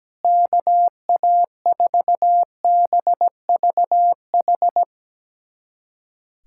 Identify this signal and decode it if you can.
Morse code, heard as a keyed tone, about 17 wpm, 700 Hz. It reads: KA4BVH
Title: USA Callsign DE Callsign - 6 Characters - 17wpm